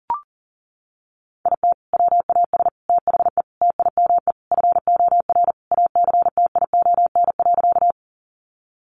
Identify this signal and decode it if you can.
Morse code, heard as a keyed tone, about 40 wpm, 700 Hz. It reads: IT WAS THE TIME FOR ACTION.
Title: Sentences from Top 700 Words 40wpm